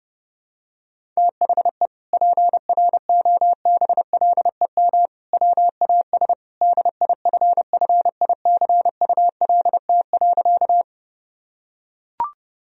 Morse code, heard as a keyed tone, about 30 wpm, 700 Hz. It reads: THE PROBLEM WAS DIFFICULT.
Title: Sentences from Top 1000 Words - Review Part 2 - 30wpm